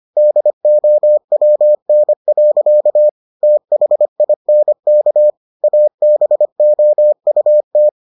Morse code, heard as a keyed tone, about 25 wpm, 600 Hz. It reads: DOWN. THINK ABOUT